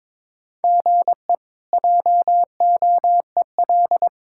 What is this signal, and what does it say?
Morse code, heard as a keyed tone, about 22 wpm, 700 Hz. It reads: GE JOEL
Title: K1USN SST QSO Practice - 22wpm